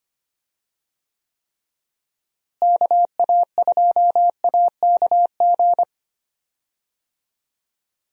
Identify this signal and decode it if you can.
Morse code, heard as a keyed tone, about 25 wpm, 700 Hz. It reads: KA2AKG
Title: USA Call Signs - 6 Characters - Repeated 3x - 25wpm